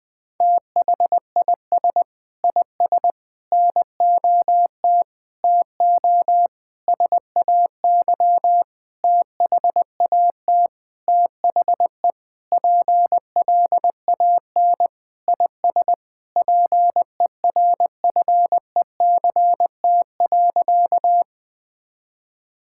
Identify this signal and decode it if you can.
Morse code, heard as a keyed tone, about 20 wpm, 700 Hz. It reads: THIS IS NOT TO SAY THAT THE PLAN IS PERFECT.
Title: Sentences with 7-Word Phrases - Spoken Beginning - 20wpm